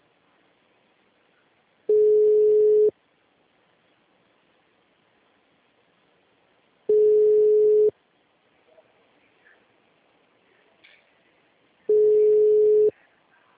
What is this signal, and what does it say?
Ringback tone (Europe, 425 Hz).